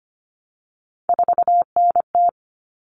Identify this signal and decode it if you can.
Morse code, heard as a keyed tone, about 25 wpm, 700 Hz. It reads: 4NT
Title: Common Cut Numbers - 3-Digit 25wpm